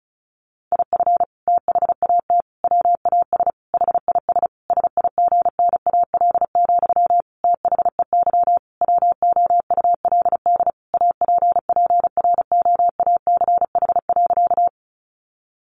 Morse code, heard as a keyed tone, about 34 wpm, 700 Hz. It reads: IF THAT WAS HIS SIGNAL, THEY WOULD APPROACH.
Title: Sentences from Top 1400 Words 35wpm